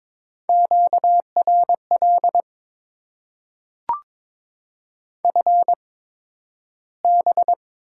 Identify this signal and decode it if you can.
Morse code, heard as a keyed tone, about 22 wpm, spaced as for 12 wpm, 700 Hz. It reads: QRL FB
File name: Top 20 QSO Elements - 8x Character Spacing - 22wpm